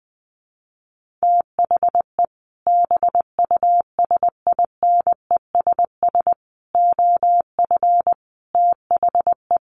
Morse code, heard as a keyed tone, about 20 wpm, 700 Hz. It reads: THE BUSINESS OF THE
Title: Sentences from Top 1500 Words 20wpm